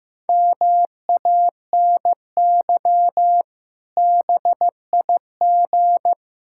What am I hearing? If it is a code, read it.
Morse code, heard as a keyed tone, about 15 wpm, 700 Hz. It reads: MANY BIG
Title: Pangram Challenge v2 - 15wpm